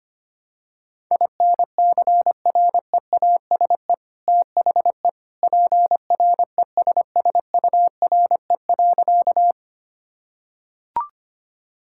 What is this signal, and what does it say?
Morse code, heard as a keyed tone, about 25 wpm, 700 Hz. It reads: INCREASE THE PRESSURE.
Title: Sentences from Top 1000 Words - Review Part 6 - 25wpm